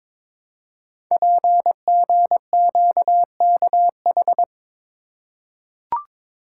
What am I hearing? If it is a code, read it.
Morse code, heard as a keyed tone, about 22 wpm, 700 Hz. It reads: PGQKH
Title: Code Groups - 5 Letters - 22wpm